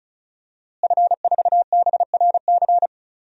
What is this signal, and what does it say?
Morse code, heard as a keyed tone, about 36 wpm, 700 Hz. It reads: F4BRC
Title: DXCC-100 Call Signs - 5 Characters - Repeated 2x - 35wpm